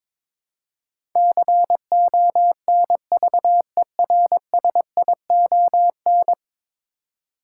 Morse code, heard as a keyed tone, about 22 wpm, 700 Hz. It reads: CONVERSION
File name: practice / 22wpm / 10 Letter Words - First 3 Letters Given - 22wpm